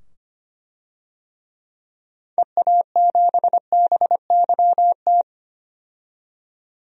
Morse code, heard as a keyed tone, about 25 wpm, 700 Hz. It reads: EA7BYT
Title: DXCC-100 Call Signs - 6 Characters - 25wpm